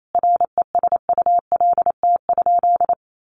Morse code, heard as a keyed tone, about 28 wpm, 700 Hz. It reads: RESULT?
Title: Sentences from Top 700 Words 28wpm